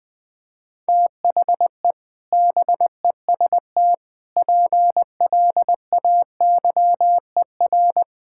Morse code, heard as a keyed tone, about 20 wpm, 700 Hz. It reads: THE BEST PLAYER
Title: Sets of 6 Words - 20wpm